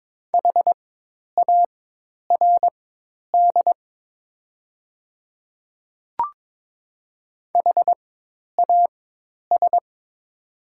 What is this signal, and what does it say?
Morse code, heard as a keyed tone, about 23 wpm, spaced as for 10 wpm, 690 Hz. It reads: HARD HAS
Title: 4x Character Spacing - Top 300 Words - 22wpm